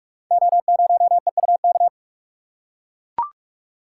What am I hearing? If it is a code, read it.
Morse code, heard as a keyed tone, about 44 wpm, 700 Hz. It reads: O0EUK
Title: Code Groups - 5 Letter:Numbers - 45wpm